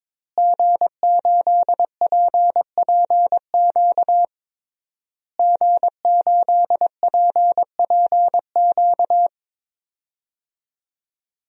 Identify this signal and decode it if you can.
Morse code, heard as a keyed tone, about 21 wpm, 700 Hz. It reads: G8PPQ G8PPQ
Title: DXCC-100 Call Signs - 5 Characters - Repeated 3x - 22wpm